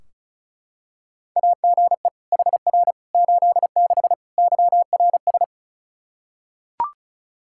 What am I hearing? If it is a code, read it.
Morse code, heard as a keyed tone, about 34 wpm, 700 Hz. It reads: AGE HR 86 YRS
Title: Practice QSO - Exchange 3 - More Info - 35wpm